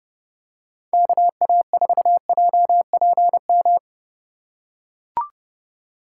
Morse code, heard as a keyed tone, about 30 wpm, 700 Hz. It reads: KA4JPM